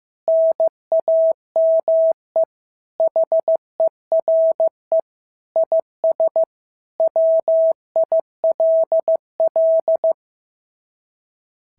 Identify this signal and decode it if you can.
Morse code, heard as a keyed tone, about 14 wpm, 650 Hz. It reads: NAME HERE IS WILL
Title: SOTA QSO Practice 15wpm